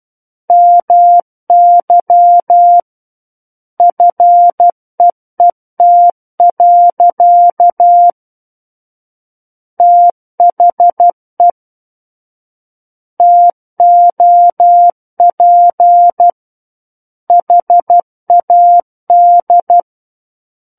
Morse code, heard as a keyed tone, about 12 wpm, 700 Hz. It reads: MY FEET. THE TOP HAD
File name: Book One / 03-Chapter 3 - 12 WPM